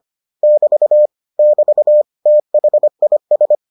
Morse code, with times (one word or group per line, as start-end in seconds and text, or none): 0.4-1.1 <BT>
1.4-2.0 <BT>
2.3-3.6 THIS